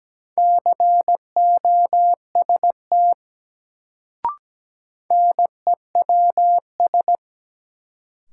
Morse code, heard as a keyed tone, about 17 wpm, 700 Hz. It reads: COST NEWS
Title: Sentences from Top 800 Words - New Word Drill - 17wpm